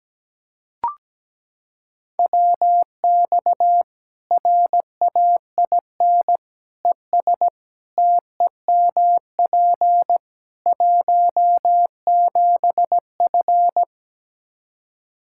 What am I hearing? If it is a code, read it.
Morse code, heard as a keyed tone, about 17 wpm, 700 Hz. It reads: WX RAIN ES TEMP 17F
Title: Practice QSO - Exchange 2 - Info - 17wpm